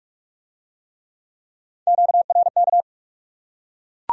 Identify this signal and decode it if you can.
Morse code, heard as a keyed tone, about 44 wpm, 700 Hz. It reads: QRK